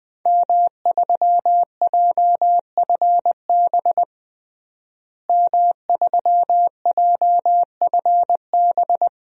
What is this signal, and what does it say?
Morse code, heard as a keyed tone, about 20 wpm, 700 Hz. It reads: M3JFB M3JFB
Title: DXCC-100 Call Signs - 5 Characters - Repeated 2x - 20wpm